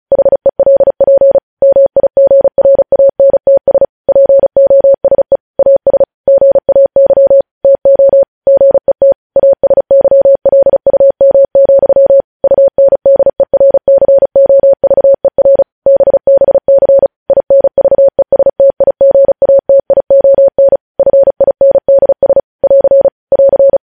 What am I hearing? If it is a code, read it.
Morse code, heard as a keyed tone, about 36 wpm, 550 Hz. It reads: HELP MIGRANTS POSE AS GAY TO GET ASYLUM, UNDERCOVER BBC INVESTIGATION FINDS <AR> <AR>